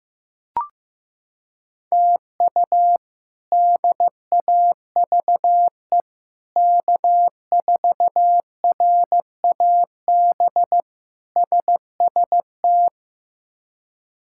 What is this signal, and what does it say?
Morse code, heard as a keyed tone, about 15 wpm, 700 Hz. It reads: TU DAVE K4RAB SST
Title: K1USN SST QSO Practice - 15wpm